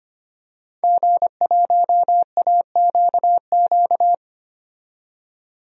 Morse code, heard as a keyed tone, about 25 wpm, 700 Hz. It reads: G1AQQ
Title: DXCC-100 Call Signs - 5 Characters - 25wpm